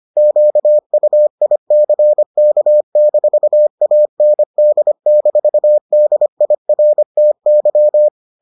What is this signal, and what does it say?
Morse code, heard as a keyed tone, about 25 wpm, 600 Hz. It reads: QUICK-AND-DIRTY